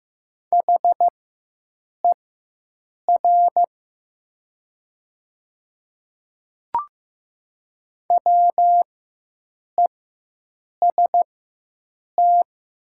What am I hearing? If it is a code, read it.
Morse code, heard as a keyed tone, about 15 wpm, spaced as for 7 wpm, 700 Hz. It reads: HER WEST
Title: 4x Character Spacing - Top 500 Words - 15wpm